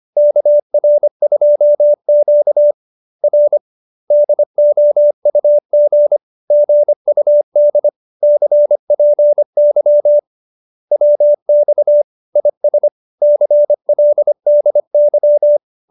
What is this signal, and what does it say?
Morse code, heard as a keyed tone, about 25 wpm, 590 Hz. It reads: KR2Q R DOUG GUD CPY WX IS CLDY